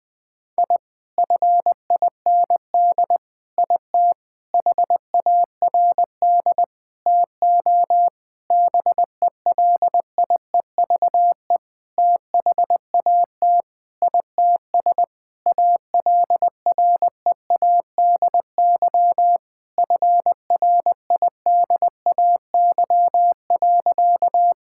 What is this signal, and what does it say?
Morse code, heard as a keyed tone, about 21 wpm, 700 Hz. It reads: I FIND IT HARD TO BELIEVE THAT ITS ALREADY FRIDAY.
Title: Sentences with 7-Word Phrases - 20wpm